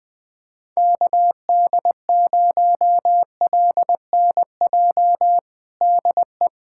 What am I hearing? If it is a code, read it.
Morse code, heard as a keyed tone, about 21 wpm, 700 Hz. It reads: KD0LNJ DE
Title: USA Callsign DE Callsign 20wpm